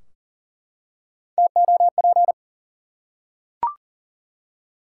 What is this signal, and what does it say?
Morse code, heard as a keyed tone, about 40 wpm, 700 Hz. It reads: TOP
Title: Three Letter Words - 40wpm